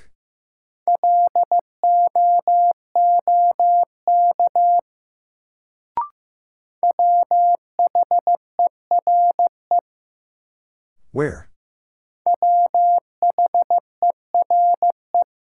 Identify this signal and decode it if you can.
Morse code, heard as a keyed tone, about 15 wpm, 700 Hz. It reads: LOOK WHERE WHERE
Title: Sets of 2 Words - New Word Drill - 15wpm